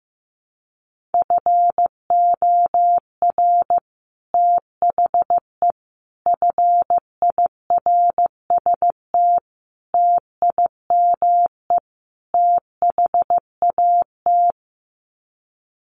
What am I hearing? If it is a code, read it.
Morse code, heard as a keyed tone, about 15 wpm, 700 Hz. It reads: FOR THE FIRST TIME THAT
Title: Sets of 5 Words Encore 15wpm